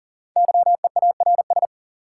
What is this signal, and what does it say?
Morse code, heard as a keyed tone, about 40 wpm, 700 Hz. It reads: YEARS
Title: Sets of 2 Words - New Word Drill - 40wpm